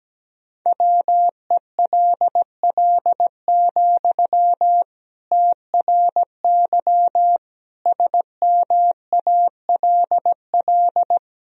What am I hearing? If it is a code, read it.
Morse code, heard as a keyed tone, about 17 wpm, 700 Hz. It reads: WELL, TRY SMALL
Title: Sentences from Top 1000 Words - Review Part 1 - 17wpm